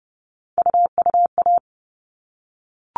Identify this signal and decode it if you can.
Morse code, heard as a keyed tone, about 30 wpm, 700 Hz. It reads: UUA
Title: All Cut Numbers - 3-Digit 30wpm